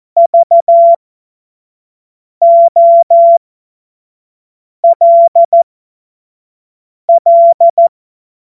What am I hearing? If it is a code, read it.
Morse code, heard as a keyed tone, about 14 wpm, spaced as for 5 wpm, 680 Hz. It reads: VOLL
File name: Witz-07-Oktober-2025-25-F70